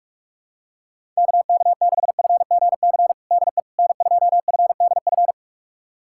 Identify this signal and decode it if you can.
Morse code, heard as a keyed tone, about 43 wpm, 700 Hz. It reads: KK6FGC DE N2FDF
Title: USA Callsign DE Callsign 45wpm